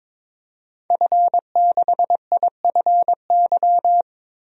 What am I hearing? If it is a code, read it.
Morse code, heard as a keyed tone, about 21 wpm, 700 Hz. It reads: F6IFY
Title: DXCC-100 Call Signs - 5 Characters - 22wpm